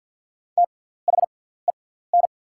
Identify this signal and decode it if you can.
Morse code, heard as a keyed tone, about 48 wpm, spaced as for 17 wpm, 690 Hz. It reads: THEN
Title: 6x Character Spacing - Top 100 Words - 50wpm